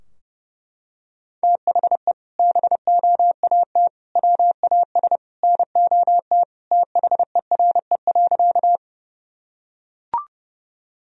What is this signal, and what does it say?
Morse code, heard as a keyed tone, about 30 wpm, 700 Hz. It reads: THE BOAT WAS NOT THERE.
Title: Sentences from Top 1000 Words - Review Part 3 - 30wpm